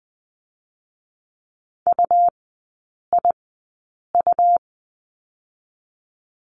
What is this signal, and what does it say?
Morse code, heard as a keyed tone, about 20 wpm, spaced as for 8 wpm, 700 Hz. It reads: UIU
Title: 127 Lesson 21e - Number 6 - Three Characters 20wpm